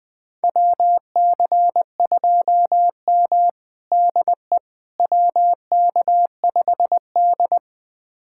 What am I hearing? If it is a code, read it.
Morse code, heard as a keyed tone, about 20 wpm, 700 Hz. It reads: WC2M DE WK5D
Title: USA Callsign DE Callsign - 4 Characters - 20wpm